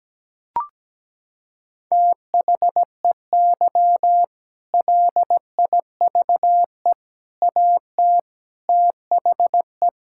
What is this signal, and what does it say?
Morse code, heard as a keyed tone, about 17 wpm, 700 Hz. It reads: THEY LIVE AT THE
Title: Sentences from Top 1000 Words - Review Part 6 - 17wpm